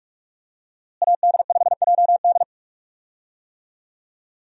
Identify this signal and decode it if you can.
Morse code, heard as a keyed tone, about 42 wpm, 700 Hz. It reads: AD5JD
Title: USA Call Signs - 5 Characters - Repeated 2x - 45wpm